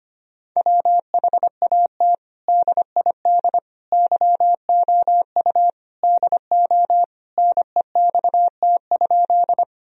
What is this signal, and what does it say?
Morse code, heard as a keyed tone, about 25 wpm, 700 Hz. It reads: WHAT DID YOU DO NEXT?
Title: Sentences from Top 1000 Words - Review Part 1 - 25wpm